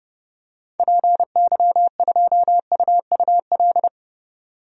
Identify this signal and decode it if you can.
Morse code, heard as a keyed tone, about 31 wpm, 700 Hz. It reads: PY2UUL
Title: DXCC-100 Call Signs - 6 Characters - 30wpm